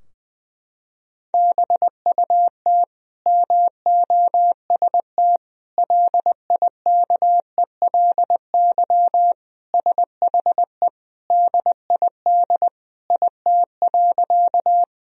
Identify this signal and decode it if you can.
Morse code, heard as a keyed tone, about 21 wpm, 700 Hz. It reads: BUT MOST LIKELY SHE DID IT.
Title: Sentences from Top 1000 Words - Review Part 4 - 20wpm